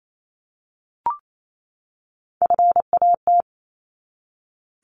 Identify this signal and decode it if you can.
Morse code, heard as a keyed tone, about 28 wpm, 700 Hz. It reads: FAT
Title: Top 501-750 Words 28wpm